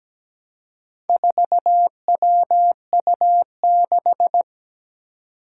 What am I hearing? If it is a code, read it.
Morse code, heard as a keyed tone, about 17 wpm, 700 Hz. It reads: 4WU6
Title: Four Challenging Letter-Number Combinations 17wpm